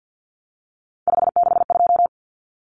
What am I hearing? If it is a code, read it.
Morse code, heard as a keyed tone, about 50 wpm, 700 Hz. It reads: 562
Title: Common Cut Numbers - 3-Digit 50wpm